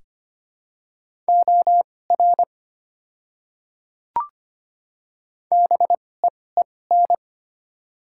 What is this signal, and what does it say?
Morse code, heard as a keyed tone, about 25 wpm, spaced as for 18 wpm, 700 Hz. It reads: OR BEEN
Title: 2x Character Spacing - Top 100 Words - 25wpm